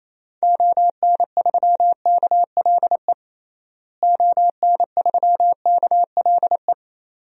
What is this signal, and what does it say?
Morse code, heard as a keyed tone, about 27 wpm, 700 Hz. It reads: ON3KLE ON3KLE
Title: DXCC-100 Call Signs - 6 Characters - Repeated 2x - 28wpm